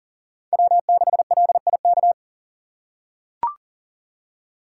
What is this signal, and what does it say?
Morse code, heard as a keyed tone, about 40 wpm, 700 Hz. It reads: W6LIK
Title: USA Call Signs - 5 Characters - 6x Character Spacing - 40wpm